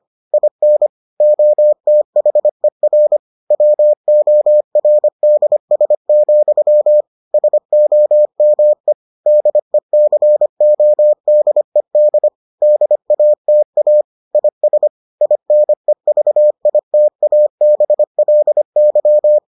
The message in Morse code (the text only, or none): IN OTHER WORDS, SOME DECODED DATA IS INEVITABLY